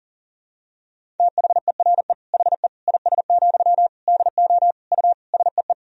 Morse code, heard as a keyed tone, about 40 wpm, 700 Hz. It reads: THERE HE IS, DO U SEE
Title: Sentences from Top 100 Words and Abbreviations 40wpm